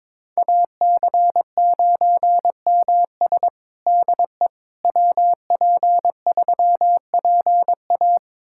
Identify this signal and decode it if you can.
Morse code, heard as a keyed tone, about 22 wpm, 700 Hz. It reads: AC9MS DE WP3PA
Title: USA Callsign DE Callsign - 5 Characters - 22wpm